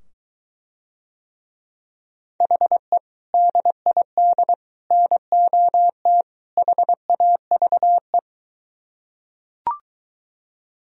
Morse code, heard as a keyed tone, about 24 wpm, 700 Hz. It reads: HE DID NOT HAVE